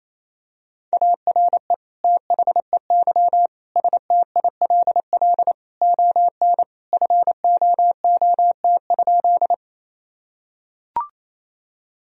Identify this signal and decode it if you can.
Morse code, heard as a keyed tone, about 28 wpm, 700 Hz. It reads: ARE THEY STILL ON FOOT?
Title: Sentences from Top 1000 Words - Review Part 3 - 28wpm